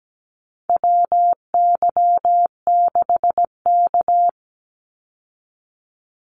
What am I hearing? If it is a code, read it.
Morse code, heard as a keyed tone, about 17 wpm, 700 Hz. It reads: WY6K
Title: Call Signs - 17wpm